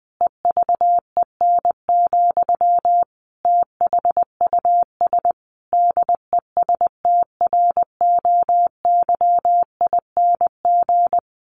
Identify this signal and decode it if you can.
Morse code, heard as a keyed tone, about 20 wpm, 700 Hz. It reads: EVEN, THUS DESTROYING
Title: A Sound Of Thunder 20wpm